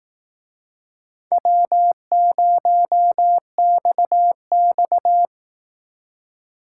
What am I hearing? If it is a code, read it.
Morse code, heard as a keyed tone, about 19 wpm, 700 Hz. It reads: W0XX
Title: Speed Racing - USA 1x2 Call Signs 15wpm